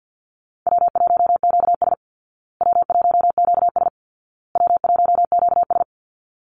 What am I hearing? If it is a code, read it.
Morse code, heard as a keyed tone, about 50 wpm, 700 Hz. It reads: W1QS W1QS W1QS
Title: Easy Call Signs Repeated 3x - 50wpm